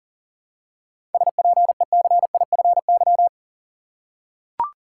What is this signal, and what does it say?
Morse code, heard as a keyed tone, about 40 wpm, 700 Hz. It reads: SPECIFY
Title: Seven Letter Words - 40wpm